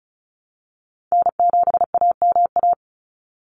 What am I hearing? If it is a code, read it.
Morse code, heard as a keyed tone, about 34 wpm, 700 Hz. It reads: N7AMA
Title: NA SOTA Chaser Callsigns 35wpm